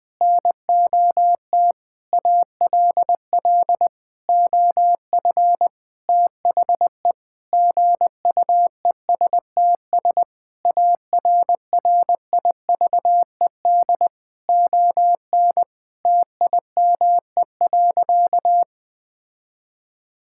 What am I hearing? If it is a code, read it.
Morse code, heard as a keyed tone, about 20 wpm, 700 Hz. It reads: NOT ALL OF THE GUESTS ARRIVED ON TIME.
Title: Sentences with 6-Word Phrases - Spoken Beginning - 20wpm